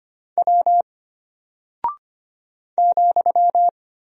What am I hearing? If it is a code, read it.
Morse code, heard as a keyed tone, about 25 wpm, 700 Hz. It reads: W ,
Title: Single Character (Letter:Number:Punctuation:Prosign) Alt 25wpm